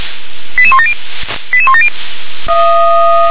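Five-tone selective call (ZVEI), heard as quick runs of five tones, tones of about 70 ms. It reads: 88170 88170
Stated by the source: Five-tone ZVEI selective call received by SDR, code 88152.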